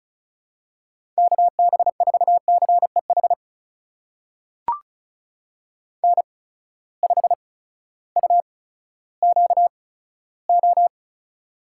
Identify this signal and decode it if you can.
Morse code, heard as a keyed tone, about 36 wpm, spaced as for 16 wpm, 700 Hz. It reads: KB4CEH N5UQO
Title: USA Call Signs - 8x Character Spacing - 35wpm